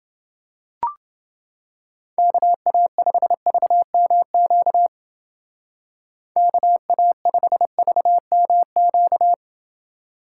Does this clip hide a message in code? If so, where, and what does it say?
Yes: Morse code from 2.2 to 9.3 s, heard as a keyed tone, about 28 wpm, 700 Hz. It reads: KA5VMQ KA5VMQ